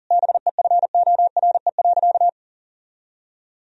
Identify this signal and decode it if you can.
Morse code, heard as a keyed tone, about 40 wpm, 700 Hz. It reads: BEFORE.
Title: Sentences from Top 1000 Words - Review Part 3 - 40wpm